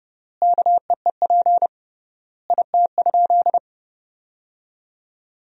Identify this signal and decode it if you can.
Morse code, heard as a keyed tone, about 32 wpm, 700 Hz. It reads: KEEP IT?
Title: 3x Word-Spacing - Sentences from Top 300 Words - 30wpm